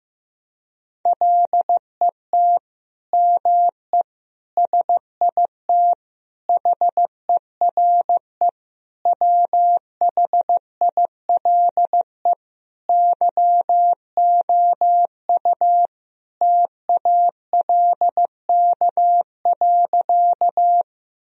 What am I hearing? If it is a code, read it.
Morse code, heard as a keyed tone, about 16 wpm, 700 Hz. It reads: LET ME SIT HERE WHILE YOU TALK.
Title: Sentences from Top 1000 Words - Review Part 1 - 15wpm